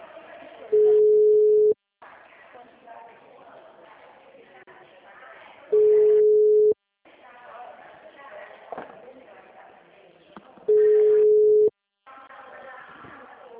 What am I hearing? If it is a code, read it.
Ringback tone (Europe, 425 Hz), with speech.